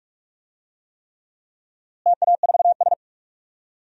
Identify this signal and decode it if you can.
Morse code, heard as a keyed tone, about 46 wpm, 700 Hz. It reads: TA4S
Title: DXCC-100 Call Signs - 4 Characters - Repeated 3x - 45wpm